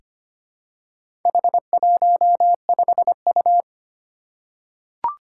Morse code, heard as a keyed tone, about 25 wpm, 700 Hz. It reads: H15U
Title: Four Challenging Letter-Number Combinations 25wpm